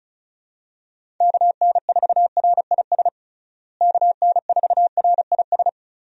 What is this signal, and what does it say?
Morse code, heard as a keyed tone, about 36 wpm, 700 Hz. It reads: KN4RIS KN4RIS
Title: USA Call Signs - 6 Characters - Repeated 3x - 35wpm